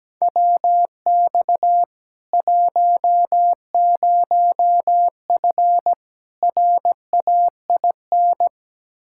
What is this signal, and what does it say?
Morse code, heard as a keyed tone, about 17 wpm, 700 Hz. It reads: WX 10F RAIN
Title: K1USN SST QSO Practice 17wpm